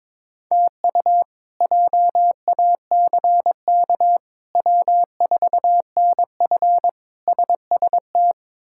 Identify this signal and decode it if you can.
Morse code, heard as a keyed tone, about 22 wpm, 700 Hz. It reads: TU JACK W4NF SST
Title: K1USN SST QSO Practice - 22wpm